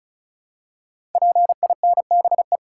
Morse code, heard as a keyed tone, about 36 wpm, 690 Hz. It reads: PINBE
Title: Code Groups - 5 Letter:Numbers - 35wpm